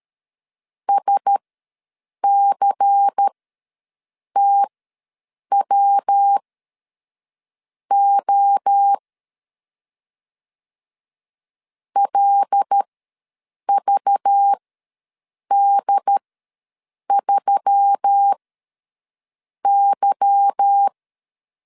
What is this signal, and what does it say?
Morse code, heard as a keyed tone, about 13 wpm, spaced as for 6 wpm, 800 Hz. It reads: SCTWO LVD3Y
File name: LEZIONI / 8 YJ45 / 8 Y ws 12wpm 3409